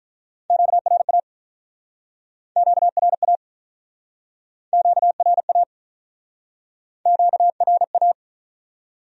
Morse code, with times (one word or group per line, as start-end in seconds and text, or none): 0.5-1.2 QRA
2.6-3.4 QRA
4.7-5.6 QRA
7.1-8.1 QRA